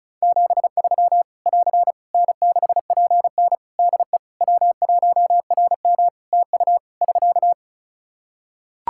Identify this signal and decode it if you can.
Morse code, heard as a keyed tone, about 36 wpm, 700 Hz. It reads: 73 <AR> N6PN DE W1RM TU <SK>
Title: Practice QSO - Exchange 4 - Ending - 35wpm